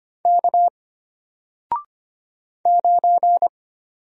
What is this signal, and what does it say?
Morse code, heard as a keyed tone, about 25 wpm, 700 Hz. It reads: K 9
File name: Single-Letter-Number-25wpm